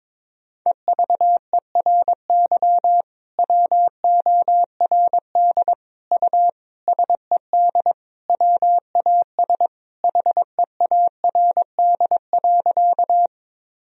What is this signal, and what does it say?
Morse code, heard as a keyed tone, about 22 wpm, 700 Hz. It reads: EVERY WORD U SED WAS HEARD.
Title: Sentences from Top 400 Words and Abbreviations 22wpm